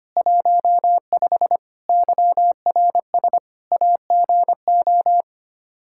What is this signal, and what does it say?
Morse code, heard as a keyed tone, about 25 wpm, 700 Hz. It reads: 15 YRS AGO
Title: K1USN SST QSO Practice 25wpm